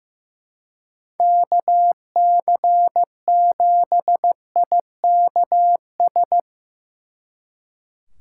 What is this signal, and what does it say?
Morse code, heard as a keyed tone, about 17 wpm, 700 Hz. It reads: KC7IKS
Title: Speed Racing - USA 2x3 Call Signs 15wpm